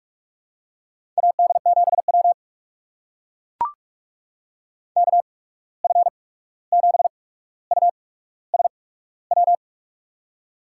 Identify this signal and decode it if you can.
Morse code, heard as a keyed tone, about 42 wpm, spaced as for 17 wpm, 700 Hz. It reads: AD7W KF7USW
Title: USA Call Signs - 8x Character Spacing - 45wpm